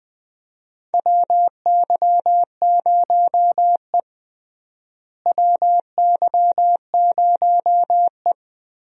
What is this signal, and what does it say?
Morse code, heard as a keyed tone, about 20 wpm, 700 Hz. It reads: WY0E WY0E
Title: USA Call Signs - 4 Characters - Repeated 3x - 20wpm